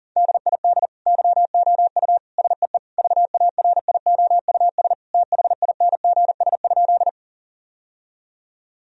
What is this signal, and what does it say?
Morse code, heard as a keyed tone, about 40 wpm, 700 Hz. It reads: DID YOU SEE VARIOUS THINGS?
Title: Sentences from Top 1000 Words - Review Part 4 - 40wpm